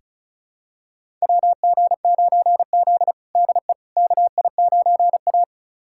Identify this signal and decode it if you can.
Morse code, heard as a keyed tone, about 34 wpm, 700 Hz. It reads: WG9Z DE KI9A
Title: USA Callsign DE Callsign - 4 Characters - 35wpm